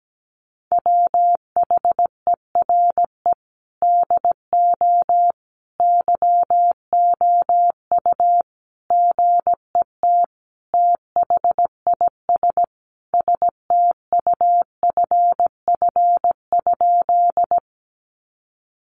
Morse code, heard as a keyed tone, about 17 wpm, 700 Hz. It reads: WHERE DO YOU GET THIS STUFF?
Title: Sentences from Top 1400 Words 17wpm